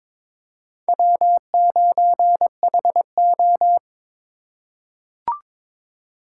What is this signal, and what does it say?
Morse code, heard as a keyed tone, about 23 wpm, 700 Hz. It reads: W9HO
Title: USA Call Signs - 4x Character Spacing - 22wpm